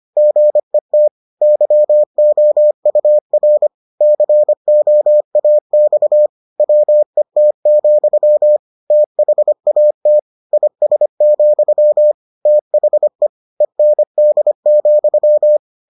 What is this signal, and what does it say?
Morse code, heard as a keyed tone, about 25 wpm, 600 Hz. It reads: GET YOUR COAX WET, THAT IS, THE END,